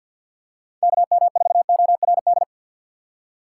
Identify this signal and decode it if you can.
Morse code, heard as a keyed tone, about 54 wpm, 700 Hz. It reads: KM4ORD
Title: USA Call Signs - 6 Characters - 8x Character Spacing - 50wpm